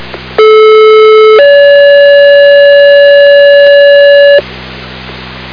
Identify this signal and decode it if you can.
Two-tone page, heard as a one-second tone, then a three-second tone, about 430 then 570 Hz, with noise. It is heard from 0.4 to 4.4 s.